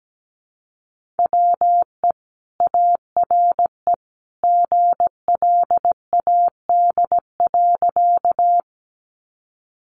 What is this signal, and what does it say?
Morse code, heard as a keyed tone, about 17 wpm, 700 Hz. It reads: WE ARE GLAD.
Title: Sentences from Top 800 Words 17wpm